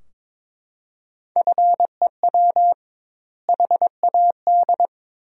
Morse code, heard as a keyed tone, about 22 wpm, 700 Hz. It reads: FEW HAD
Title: 2x Word-Spacing - Sentences from Top 300 Words - 22wpm